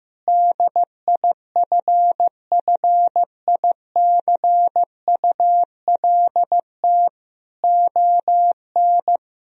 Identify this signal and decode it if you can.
Morse code, heard as a keyed tone, about 15 wpm, 700 Hz. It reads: DIFFICULT ON
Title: Sentences from Top 1000 Words - Review Part 3 - 15wpm